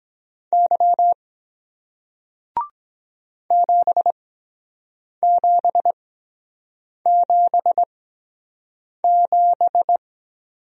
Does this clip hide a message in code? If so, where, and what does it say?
Yes: Morse code from 0.5 to 10.0 s, heard as a keyed tone, about 24 wpm, spaced as for 7 wpm, 700 Hz. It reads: Y 7777